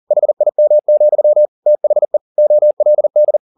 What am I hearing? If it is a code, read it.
Morse code, heard as a keyed tone, about 40 wpm, 600 Hz. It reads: HIM, THE OLD
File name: the old man and the sea 2 40wpm